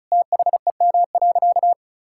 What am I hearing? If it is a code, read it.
Morse code, heard as a keyed tone, about 36 wpm, 700 Hz. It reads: THEM.
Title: Sentences from Top 1000 Words - Review Part 5 - 35wpm